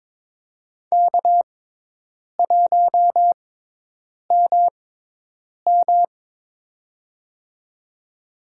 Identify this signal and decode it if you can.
Morse code, heard as a keyed tone, about 22 wpm, spaced as for 8 wpm, 700 Hz. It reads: K1MM
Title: USA Call Signs - 4 Characters - 6x Character Spacing - 22wpm